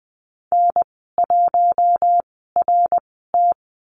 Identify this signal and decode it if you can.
Morse code, heard as a keyed tone, about 20 wpm, spaced as for 15 wpm, 700 Hz. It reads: N1RT
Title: 082 Lesson 14g - Number 2 - Call Signs 20wpm